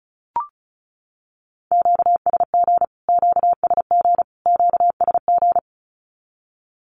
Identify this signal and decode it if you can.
Morse code, heard as a keyed tone, about 34 wpm, 700 Hz. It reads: QSG QSG QSG
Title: Q-Codes Repeated 3x 35wpm